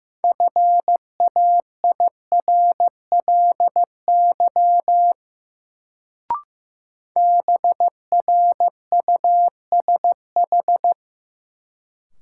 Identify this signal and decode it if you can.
Morse code, heard as a keyed tone, about 15 wpm, 700 Hz. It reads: FAIRLY BRUSH
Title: Sentences from Top 1100 Words - New Word Drill Alt - 15wpm